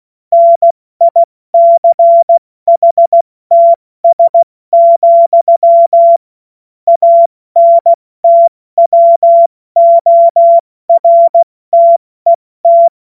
Morse code, heard as a keyed tone, about 16 wpm, 680 Hz. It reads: NICHTS, ANTWORTET
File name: Morsekurs / Witze / Witz 27. Januar 2026 70 F80